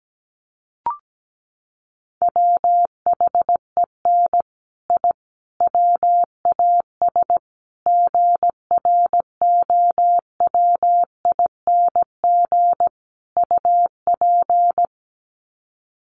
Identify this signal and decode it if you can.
Morse code, heard as a keyed tone, about 17 wpm, 700 Hz. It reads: WHEN I WAS GROWING UP
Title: Sets of 5 Words Encore 17wpm